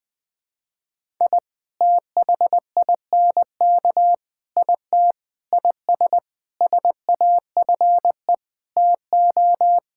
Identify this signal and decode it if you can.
Morse code, heard as a keyed tone, about 20 wpm, 700 Hz. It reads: I THINK IT IS SAFE TO
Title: Sentences with 8-Word Phrases - 20wpm